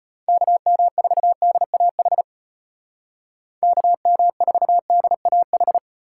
Morse code, heard as a keyed tone, about 33 wpm, 700 Hz. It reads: KM4DAH KM4DAH